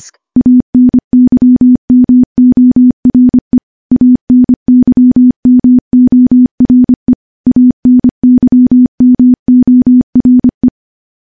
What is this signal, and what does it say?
Morse code, heard as a keyed tone, about 25 wpm, 250 Hz. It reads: ANYMORE ANYMORE ANYMORE
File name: morse / morse-training-25WPM